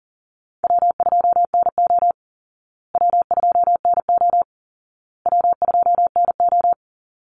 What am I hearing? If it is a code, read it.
Morse code, heard as a keyed tone, about 40 wpm, 700 Hz. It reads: W2NO W2NO W2NO
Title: Easy Call Signs Repeated 3x - 40wpm